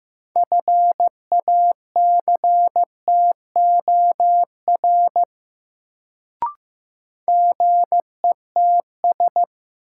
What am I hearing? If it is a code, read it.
Morse code, heard as a keyed tone, about 15 wpm, 700 Hz. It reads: FACTOR GETS
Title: Sentences from Top 1300 Words - New Word Drill Alt - 15wpm